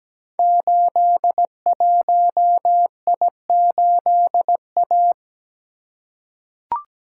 Morse code, heard as a keyed tone, about 17 wpm, 700 Hz. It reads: 81I8A
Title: Code Groups - 5 Letter:Numbers - 17wpm